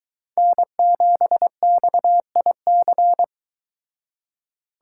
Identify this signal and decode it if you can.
Morse code, heard as a keyed tone, about 22 wpm, 700 Hz. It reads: N7XIC